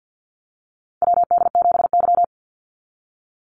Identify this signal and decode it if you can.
Morse code, heard as a keyed tone, about 50 wpm, 700 Hz. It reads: WD7Y
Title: NA SOTA Chaser Callsigns 50wpm